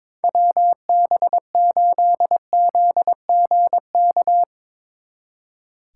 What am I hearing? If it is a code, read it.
Morse code, heard as a keyed tone, about 27 wpm, 700 Hz. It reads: WB8ZGK
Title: Speed Racing - USA 2x3 Call Signs 22wpm